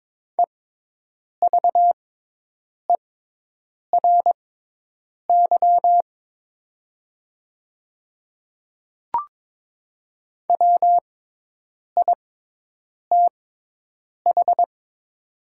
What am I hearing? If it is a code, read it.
Morse code, heard as a keyed tone, about 22 wpm, spaced as for 8 wpm, 700 Hz. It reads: EVERY WITH
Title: 6x Character Spacing - Top 200 Words - 22wpm